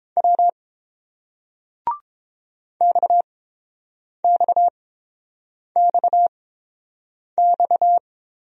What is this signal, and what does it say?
Morse code, heard as a keyed tone, about 30 wpm, spaced as for 7 wpm, 690 Hz. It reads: W XXXX